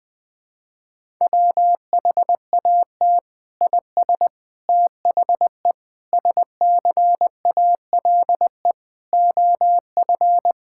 Morse code, heard as a keyed tone, about 21 wpm, 700 Hz. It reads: WHAT IS THE SCALE OF
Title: Sentences from Top 1000 Words - Review Part 5 - 20wpm